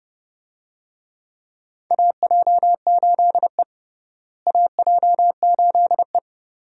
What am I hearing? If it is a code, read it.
Morse code, heard as a keyed tone, about 30 wpm, 700 Hz. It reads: AJ8E AJ8E
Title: USA Call Signs - 4 Characters - Repeated 2x - 30wpm